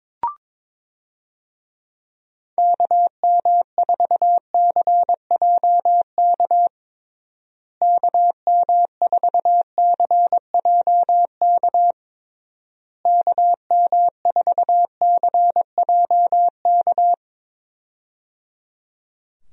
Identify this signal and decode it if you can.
Morse code, heard as a keyed tone, about 22 wpm, 700 Hz. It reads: KM4CJK KM4CJK KM4CJK
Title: USA Call Signs - 6 Characters - Repeated 3x - 22wpm